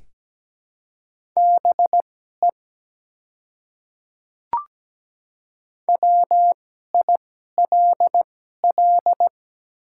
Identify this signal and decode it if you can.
Morse code, heard as a keyed tone, about 17 wpm, spaced as for 12 wpm, 700 Hz. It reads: BE WILL
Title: 2x Character Spacing - Top 200 Words - 17wpm